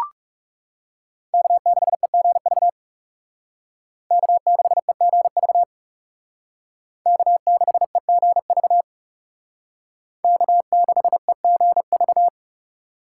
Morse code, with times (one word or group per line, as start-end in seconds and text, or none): 1.3-2.7 K6EGV
4.1-5.6 K6EGV
7.1-8.8 K6EGV
10.2-12.3 K6EGV